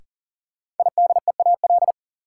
Morse code, heard as a keyed tone, about 37 wpm, 700 Hz. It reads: IDEAL
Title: Sentences from Top 1400 Words - New Word Drill Alt - 40wpm